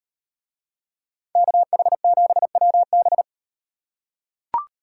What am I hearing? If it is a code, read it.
Morse code, heard as a keyed tone, about 38 wpm, 700 Hz. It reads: KH7WB